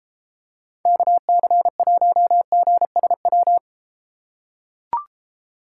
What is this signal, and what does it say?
Morse code, heard as a keyed tone, about 31 wpm, 700 Hz. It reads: KC1GSW